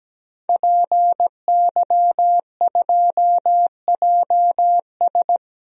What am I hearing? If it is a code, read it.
Morse code, heard as a keyed tone, about 17 wpm, 700 Hz. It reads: PY2JS
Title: DXCC-100 Call Signs - 5 Characters - 17wpm